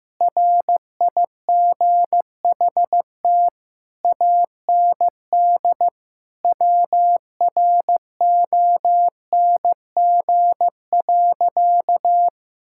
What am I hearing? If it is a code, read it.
Morse code, heard as a keyed tone, about 16 wpm, 700 Hz. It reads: RIGHT AND WRONG.
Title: Sentences from Top 1600 Words 15wpm